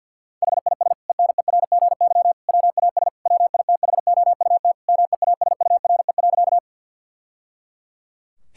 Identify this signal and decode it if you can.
Morse code, heard as a keyed tone, about 50 wpm, 700 Hz. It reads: HIS ENERGY WAS WITHOUT MEASURE.
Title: Sentences from Top 1000 Words - Review Part 6 - 50wpm